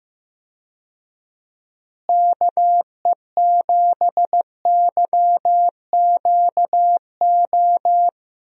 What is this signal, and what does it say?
Morse code, heard as a keyed tone, about 15 wpm, 700 Hz. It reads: KE7YQO
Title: USA Call Signs - 6 Characters - Repeated 2x - 15wpm